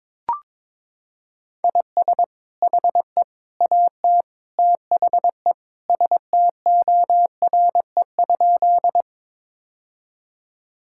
Morse code, heard as a keyed tone, about 22 wpm, 700 Hz. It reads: IS HE AT THE STORE?
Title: Sentences from Top 1000 Words - Review Part 5 - 22wpm